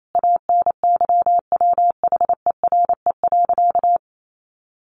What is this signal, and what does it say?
Morse code, heard as a keyed tone, about 28 wpm, 700 Hz. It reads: ANYWHERE.
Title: Sentences from Top 1400 Words 28wpm